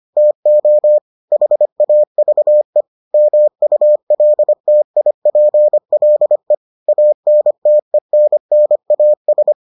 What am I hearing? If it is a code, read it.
Morse code, heard as a keyed tone, about 25 wpm, 600 Hz. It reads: TO HAVE MULTIPLE ANTENNAS